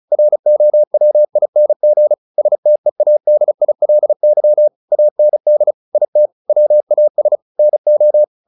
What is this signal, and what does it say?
Morse code, heard as a keyed tone, about 34 wpm, 600 Hz. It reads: ROWING STEADILY AND IT WAS NO
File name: the old man and the sea / the old man and the sea 1 35wpm